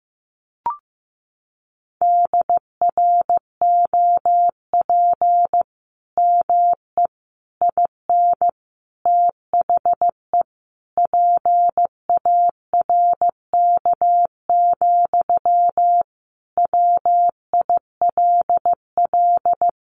Morse code, heard as a keyed tone, about 15 wpm, 700 Hz. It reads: DROP ME IN THE PARK, WILL
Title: Sentences from Top 1400 Words 15wpm